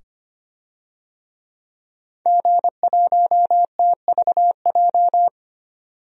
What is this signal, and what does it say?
Morse code, heard as a keyed tone, about 25 wpm, 700 Hz. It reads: G1TVJ
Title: DXCC-100 Call Signs - 5 Characters - 25wpm